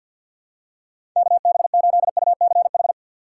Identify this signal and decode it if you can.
Morse code, heard as a keyed tone, about 53 wpm, 700 Hz. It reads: KB8UCH
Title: USA Call Signs - 6 Characters - 8x Character Spacing - 50wpm